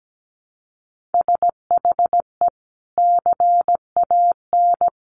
Morse code, heard as a keyed tone, about 17 wpm, 700 Hz. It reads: SHE CAN
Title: Sentences from Top 1000 Words 17wpm